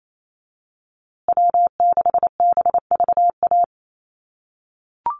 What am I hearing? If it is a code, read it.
Morse code, heard as a keyed tone, about 28 wpm, 700 Hz. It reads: W6BVA
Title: USA Call Signs Repeated 2x 28wpm